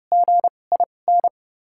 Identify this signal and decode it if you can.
Morse code, heard as a keyed tone, about 30 wpm, spaced as for 22 wpm, 700 Hz. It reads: GIN
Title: 2x Character Spacing - Top 500 Words - 30wpm